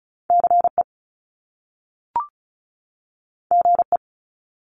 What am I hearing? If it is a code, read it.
Morse code, heard as a keyed tone, about 36 wpm, 700 Hz. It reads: CE GE
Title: Top Two Letter Combinations in Words 35wpm